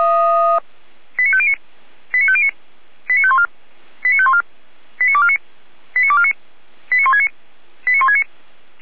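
Five-tone selective call (ZVEI), heard as quick runs of five tones, tones of about 70 ms. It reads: 89509 89509 89514 89514 89249 89249 89169 89169
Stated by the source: ZVEI five-tone call received on an SDR.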